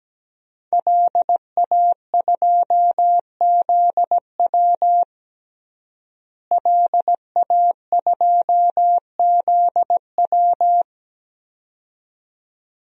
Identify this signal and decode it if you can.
Morse code, heard as a keyed tone, about 17 wpm, 700 Hz. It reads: LA2ZW LA2ZW
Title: DXCC-100 Call Signs - 5 Characters - Repeated 2x - 17wpm